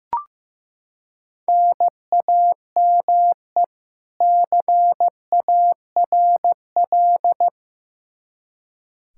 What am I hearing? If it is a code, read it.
Morse code, heard as a keyed tone, about 15 wpm, 700 Hz. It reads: NAME CARL
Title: Practice QSO - Exchange 1 - Beginning - 15wpm